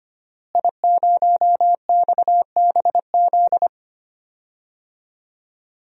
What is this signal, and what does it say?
Morse code, heard as a keyed tone, about 25 wpm, 700 Hz. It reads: I0XBZ
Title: DXCC-100 Call Signs - 5 Characters - Repeated 2x - 25wpm